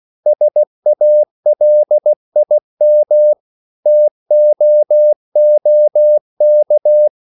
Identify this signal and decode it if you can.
Morse code, heard as a keyed tone, about 16 wpm, 600 Hz. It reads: SALIM TOOK